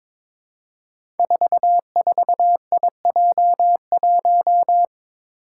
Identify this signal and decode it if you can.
Morse code, heard as a keyed tone, about 22 wpm, 700 Hz. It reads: 44IJ1
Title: Five Challenging Letter-Number Combinations 22wpm